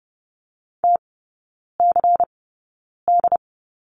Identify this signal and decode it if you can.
Morse code, heard as a keyed tone, about 30 wpm, spaced as for 9 wpm, 700 Hz. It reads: TCD
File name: Lesson 16e - Letter c - Three Characters 30wpm